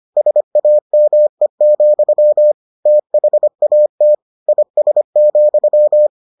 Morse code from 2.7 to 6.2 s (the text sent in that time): THAT IS,